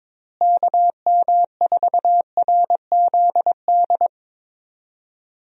Morse code, heard as a keyed tone, about 22 wpm, 700 Hz. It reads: KM4RZD
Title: USA Call Signs - 6 Characters - Repeated 2x - 22wpm